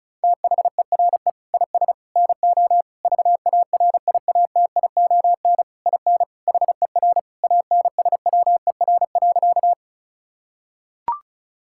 Morse code, heard as a keyed tone, about 34 wpm, 700 Hz. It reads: THERE IS NO VARIATION IN HER ANSWER.
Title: Sentences from Top 1600 Words 35wpm